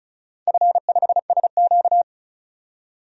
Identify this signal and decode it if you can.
Morse code, heard as a keyed tone, about 36 wpm, 700 Hz. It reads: F5SQ
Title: DXCC-100 Call Signs - 4 Characters - 35wpm